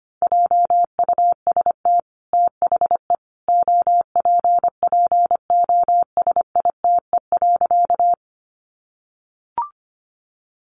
Morse code, heard as a keyed tone, about 25 wpm, 700 Hz. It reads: JUST THE OPPOSITE.
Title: Sentences from Top 1300 Words 25wpm